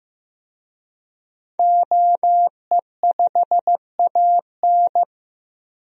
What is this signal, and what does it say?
Morse code, heard as a keyed tone, about 15 wpm, 700 Hz. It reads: OE5AN
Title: DXCC-100 Call Signs - 5 Characters - 15wpm